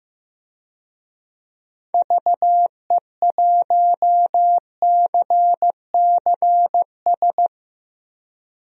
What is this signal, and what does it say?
Morse code, heard as a keyed tone, about 15 wpm, 700 Hz. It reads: VE1CCS
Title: DXCC-100 Call Signs - 6 Characters - Repeated 2x - 15wpm